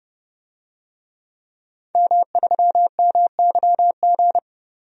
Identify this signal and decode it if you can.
Morse code, heard as a keyed tone, about 30 wpm, 700 Hz. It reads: M3MYG
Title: DXCC-100 Call Signs - 5 Characters - 30wpm